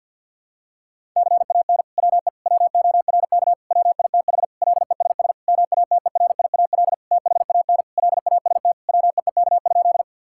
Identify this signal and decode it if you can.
Morse code, heard as a keyed tone, about 46 wpm, 700 Hz. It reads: CAN WE WORK WITH LESS MATERIAL THAN LAST WEEK?
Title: Sentences from Top 1000 Words - Review Part 5 - 50wpm